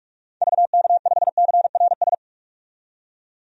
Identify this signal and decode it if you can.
Morse code, heard as a keyed tone, about 48 wpm, 700 Hz. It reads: VK5CRS
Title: DXCC-100 Call Signs - 6 Characters - 45wpm